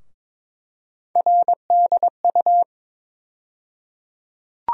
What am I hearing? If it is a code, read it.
Morse code, heard as a keyed tone, about 22 wpm, 700 Hz. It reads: RDU
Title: USA Call Sign Postfixes - 3 Letters 22wpm